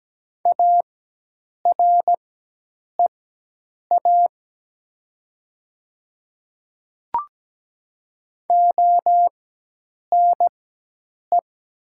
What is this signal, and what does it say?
Morse code, heard as a keyed tone, about 17 wpm, spaced as for 8 wpm, 700 Hz. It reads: AREA ONE
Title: 4x Character Spacing - Top 500 Words - 17wpm